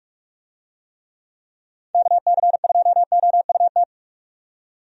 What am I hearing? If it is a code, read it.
Morse code, heard as a keyed tone, about 46 wpm, 700 Hz. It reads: KC2OUT
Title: USA Call Signs - 6 Characters - 45wpm